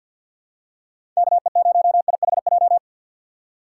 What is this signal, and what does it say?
Morse code, heard as a keyed tone, about 51 wpm, 700 Hz. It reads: KE0IHJ
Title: USA Call Signs - 6 Characters - 6x Character Spacing - 50wpm